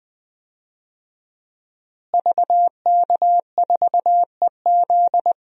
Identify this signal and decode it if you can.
Morse code, heard as a keyed tone, about 20 wpm, 700 Hz. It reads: VK4EZ
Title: DXCC-100 Call Signs - 5 Characters - Repeated 3x - 20wpm